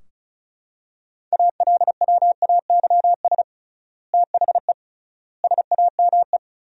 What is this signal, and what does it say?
Morse code, heard as a keyed tone, about 36 wpm, 700 Hz. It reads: ALWAYS THE SAME
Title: 3x Word-Spacing - Sentences from Top 300 Words - 35wpm